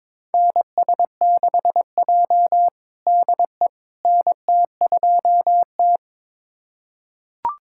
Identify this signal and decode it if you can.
Morse code, heard as a keyed tone, about 22 wpm, 700 Hz. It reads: NS6J DE NT2T
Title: USA Callsign DE Callsign - 4 Characters - 22wpm